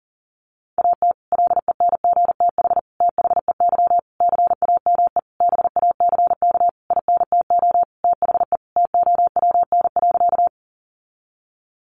Morse code, heard as a keyed tone, about 40 wpm, 700 Hz. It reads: AT LENGTH THEY CAME BACK INTO THE TOWN.
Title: Sentences from Top 1000 Words 40wpm